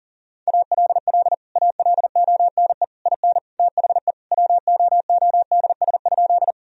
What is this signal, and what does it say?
Morse code, heard as a keyed tone, about 40 wpm, 700 Hz. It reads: ALL ALONE IN THE WOODS?
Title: Sentences from Top 1000 Words - Review Part 2 - 40wpm